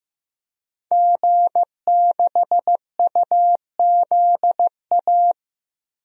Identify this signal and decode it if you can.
Morse code, heard as a keyed tone, about 16 wpm, 700 Hz. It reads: G6UZA
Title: DXCC-100 Call Signs - 5 Characters - Repeated 2x - 15wpm